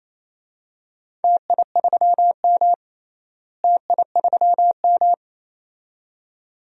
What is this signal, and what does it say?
Morse code, heard as a keyed tone, about 28 wpm, 700 Hz. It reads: TI3M TI3M
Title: DXCC-100 Call Signs - 4 Characters - Repeated 2x - 28wpm